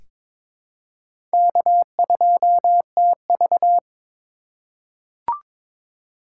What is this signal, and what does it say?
Morse code, heard as a keyed tone, about 22 wpm, 700 Hz. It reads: K2TV
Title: K1USN SST QSO Practice - 22wpm